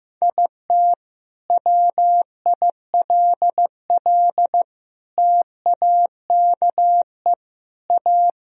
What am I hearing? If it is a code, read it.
Morse code, heard as a keyed tone, about 16 wpm, 700 Hz. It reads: IT WILL TAKE A
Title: Sentences from Top 1000 Words - Review Part 4 - 15wpm